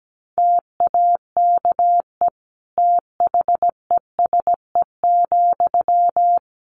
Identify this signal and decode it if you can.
Morse code, heard as a keyed tone, about 17 wpm, 700 Hz. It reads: TAKE THESE,
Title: Sentences from Top 300 Words 17wpm